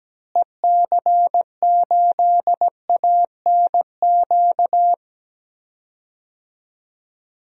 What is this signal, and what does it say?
Morse code, heard as a keyed tone, about 17 wpm, 700 Hz. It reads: EC8ANQ
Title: DXCC-100 Call Signs - 6 Characters - Repeated 2x - 17wpm